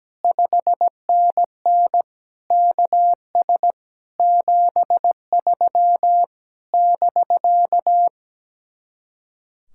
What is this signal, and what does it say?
Morse code, heard as a keyed tone, about 17 wpm, 700 Hz. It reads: 5NN KS 73 <BK>
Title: Speed-Racing Common QSO Phrases Encore - 17wpm